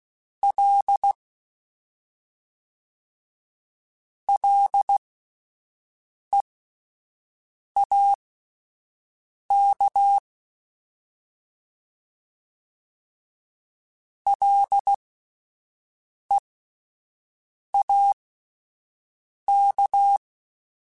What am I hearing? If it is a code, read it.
Morse code, heard as a keyed tone, about 16 wpm, spaced as for 5 wpm, 800 Hz. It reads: L LEAK LEAK